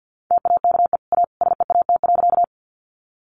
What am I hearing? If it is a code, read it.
Morse code, heard as a keyed tone, about 50 wpm, 700 Hz. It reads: TAKE A SEAT.
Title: Sentences from Top 1500 Words 50wpm